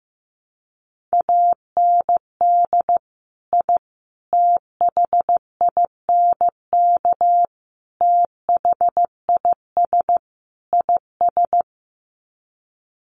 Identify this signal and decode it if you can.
Morse code, heard as a keyed tone, about 15 wpm, 700 Hz. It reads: AND I THINK THIS IS
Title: Sets of 5 Words Encore 15wpm